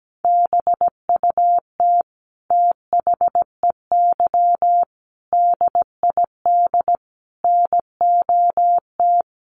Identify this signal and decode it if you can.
Morse code, heard as a keyed tone, about 17 wpm, 700 Hz. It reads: BUT THEY DID NOT
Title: Sentences from Top 1400 Words 17wpm